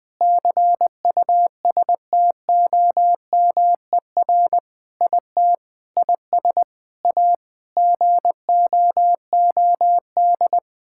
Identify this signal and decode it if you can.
Morse code, heard as a keyed tone, about 20 wpm, 700 Hz. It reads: CUSTOMER IT IS A GOOD
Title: Sentences with 7-Word Phrases - 20wpm